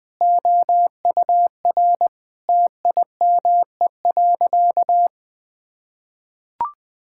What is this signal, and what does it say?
Morse code, heard as a keyed tone, about 21 wpm, 700 Hz. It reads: OUR TIME.
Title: Sentences with 4-Word Phrases - 20wpm